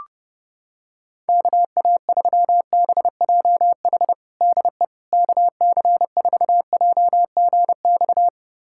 Morse code, heard as a keyed tone, about 32 wpm, 700 Hz. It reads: KA3BJH DE KC4JGX
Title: USA Callsign DE Callsign - 6 Characters - 30wpm